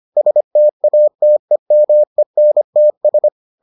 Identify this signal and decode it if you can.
Morse code, heard as a keyed tone, about 25 wpm, 600 Hz. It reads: STATEMENTS